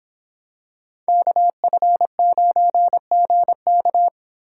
Morse code, heard as a keyed tone, about 26 wpm, 700 Hz. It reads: KF9GK